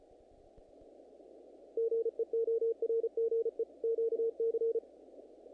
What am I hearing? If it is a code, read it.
Morse code, heard as a keyed tone, about 34 wpm, 460 Hz. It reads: GEORGE QC
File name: contests / naqp-cw-k5tr-op-k5pi- -1900